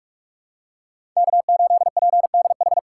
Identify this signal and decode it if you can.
Morse code, heard as a keyed tone, about 46 wpm, 700 Hz. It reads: K8PDH
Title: USA Call Signs - 5 Characters - 6x Character Spacing - 45wpm